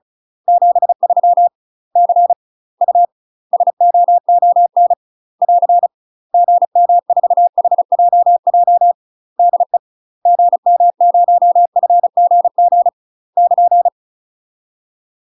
Morse code, heard as a keyed tone, about 35 wpm, 700 Hz. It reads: 73 C U SOON <AR> GM4HJJ DE GM0FGG <KN>
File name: UK MASTER QSOs TRAINING/35WPM